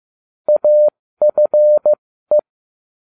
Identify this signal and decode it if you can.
Morse code, heard as a keyed tone, about 15 wpm, spaced as for 13 wpm, 600 Hz. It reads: AFE